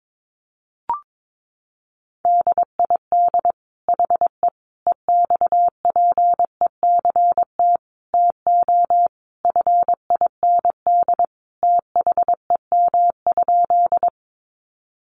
Morse code, heard as a keyed tone, about 22 wpm, 700 Hz. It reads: DID HE EXPECT TO FIND THEM?
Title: Sentences from Top 1100 Words 22wpm